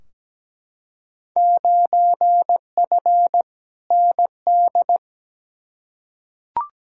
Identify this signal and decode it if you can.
Morse code, heard as a keyed tone, about 17 wpm, 700 Hz. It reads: 9F ND
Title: Field Day Exchanges Encore - 17wpm